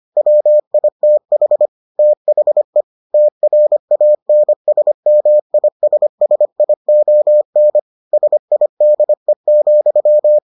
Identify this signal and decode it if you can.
Morse code, heard as a keyed tone, about 25 wpm, 600 Hz. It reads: WITH THE TRANSMISSION SIDE,